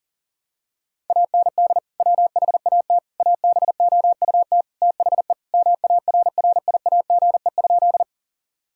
Morse code, heard as a keyed tone, about 40 wpm, 700 Hz. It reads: AND WHAT ABOUT THE MARRIAGE?
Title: Sentences from Top 1000 Words - Review Part 5 - 40wpm